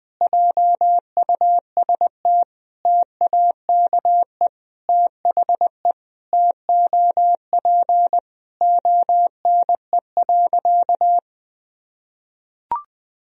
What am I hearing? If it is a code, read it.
Morse code, heard as a keyed tone, about 20 wpm, 700 Hz. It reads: JUST TAKE THE TOP ONE.
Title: Sentences from Top 1000 Words - Review Part 2 - 20wpm